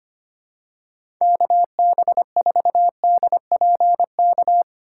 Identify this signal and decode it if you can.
Morse code, heard as a keyed tone, about 25 wpm, 700 Hz. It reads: KB4DPK
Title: USA Call Signs - 2x Character Spacing - 25wpm